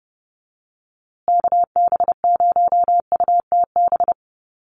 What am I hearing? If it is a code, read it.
Morse code, heard as a keyed tone, about 30 wpm, 700 Hz. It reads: KB0UTB
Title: USA Call Signs Repeated 3x 30wpm